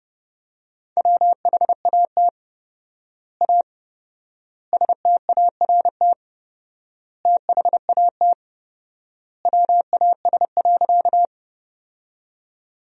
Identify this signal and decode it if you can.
Morse code, heard as a keyed tone, about 30 wpm, 700 Hz. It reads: WHAT A START THAT WAS.
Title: 4x Word-Spacing - Sentences from Top 300 Words - 30wpm